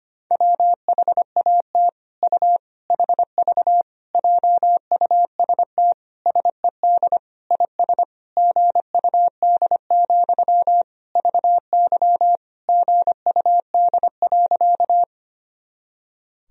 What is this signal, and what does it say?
Morse code, heard as a keyed tone, about 25 wpm, 700 Hz. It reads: WHAT U HV JUST SED IS GUD, VY GUD.
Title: Sentences from Top 200 Words and Abbreviations 25wpm